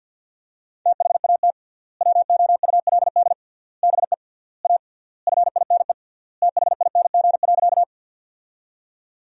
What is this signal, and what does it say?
Morse code, heard as a keyed tone, about 50 wpm, 700 Hz. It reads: THAT WOULD BE A FINE THING.
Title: 3x Word-Spacing - Sentences from Top 500 Words - 50wpm